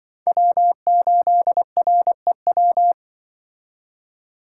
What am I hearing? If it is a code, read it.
Morse code, heard as a keyed tone, about 24 wpm, 700 Hz. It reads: W8REW
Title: Speed Racing - USA 1x3 Call Signs 20wpm